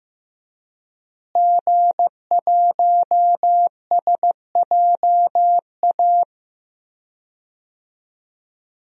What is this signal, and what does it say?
Morse code, heard as a keyed tone, about 15 wpm, 700 Hz. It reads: G1SJA
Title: DXCC-100 Call Signs - 5 Characters - Repeated 2x - 15wpm